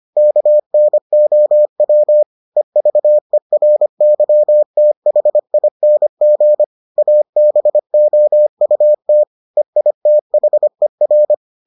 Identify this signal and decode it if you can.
Morse code, heard as a keyed tone, about 25 wpm, 600 Hz. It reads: KNOW EVERYTHING ABOUT EITHER